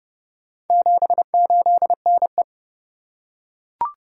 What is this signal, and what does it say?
Morse code, heard as a keyed tone, about 27 wpm, 700 Hz. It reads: 78NE